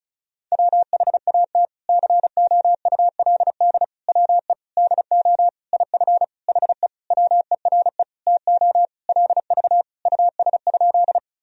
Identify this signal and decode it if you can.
Morse code, heard as a keyed tone, about 36 wpm, 700 Hz. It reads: WHAT COULD WE DO IF HE WERE TO LV US?
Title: Sentences from Top 400 Words and Abbreviations 35wpm